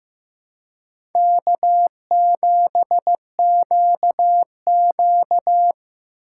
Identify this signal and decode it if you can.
Morse code, heard as a keyed tone, about 15 wpm, 700 Hz. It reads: K7QQ
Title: USA Call Signs - 4 Characters - 8x Character Spacing - 15wpm